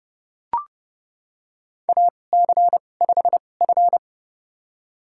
Morse code, heard as a keyed tone, about 30 wpm, spaced as for 22 wpm, 700 Hz. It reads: AC5F
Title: USA Call Signs - 2x Character Spacing - 30wpm